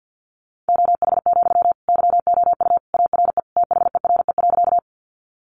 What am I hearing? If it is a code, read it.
Morse code, heard as a keyed tone, about 50 wpm, 700 Hz. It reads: OH, YOU ARE THERE.
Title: Sentences from Top 500 Words 50wpm